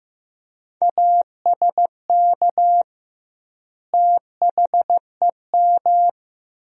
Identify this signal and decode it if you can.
Morse code, heard as a keyed tone, about 15 wpm, 700 Hz. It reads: ASK THEM
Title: 2x Word-Spacing - Sentences from Top 400 Words - 15wpm